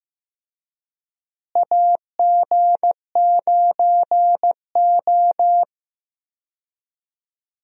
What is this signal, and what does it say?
Morse code, heard as a keyed tone, about 15 wpm, 700 Hz. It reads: AG9O
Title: USA Call Signs - 4 Characters - Repeated 2x - 15wpm